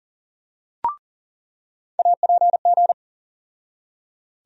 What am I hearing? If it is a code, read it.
Morse code, heard as a keyed tone, about 40 wpm, 700 Hz. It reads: APG
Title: USA Call Sign Postfixes - 3 Letters 40wpm